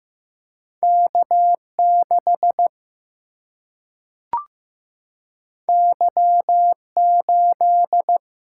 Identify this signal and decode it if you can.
Morse code, heard as a keyed tone, about 15 wpm, 700 Hz. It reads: K 6 Y 8
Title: Single-Letter Single-Number 15wpm